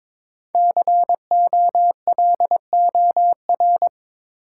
Morse code, heard as a keyed tone, about 22 wpm, 700 Hz. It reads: COLOR
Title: Sentences from Top 1000 Words - Review Part 2 - New Word Drill - 22wpm